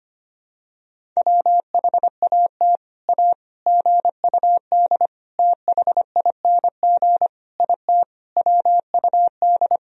Morse code, heard as a keyed tone, about 25 wpm, 700 Hz. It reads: WHAT A GUD THING IT WUD
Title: Sentences from Top 200 Words and Abbreviations 25wpm